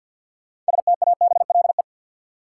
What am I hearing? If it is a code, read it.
Morse code, heard as a keyed tone, about 53 wpm, 700 Hz. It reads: STABLE
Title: Sentences from Top 1500 Words - New Word Drill - 50wpm